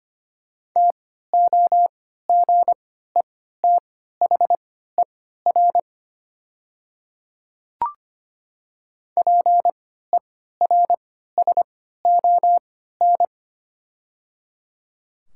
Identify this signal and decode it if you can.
Morse code, heard as a keyed tone, about 25 wpm, spaced as for 14 wpm, 700 Hz. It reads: TOGETHER PERSON
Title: 3x Character Spacing - Top 500 Words - 25wpm